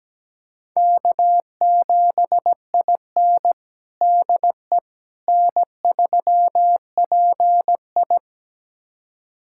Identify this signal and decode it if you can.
Morse code, heard as a keyed tone, about 17 wpm, 700 Hz. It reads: K7IN DE N3PI
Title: USA Callsign DE Callsign - 4 Characters - 17wpm